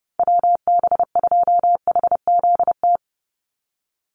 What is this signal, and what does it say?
Morse code, heard as a keyed tone, about 30 wpm, 700 Hz. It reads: WB2HZT
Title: USA Call Signs 30wpm